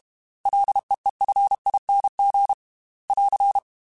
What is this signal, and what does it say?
Morse code, heard as a keyed tone, about 32 wpm, 800 Hz. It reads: LEEFING <AR>
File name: Ls / Single L m3s / Single L Prefixe 3x2 Twice @30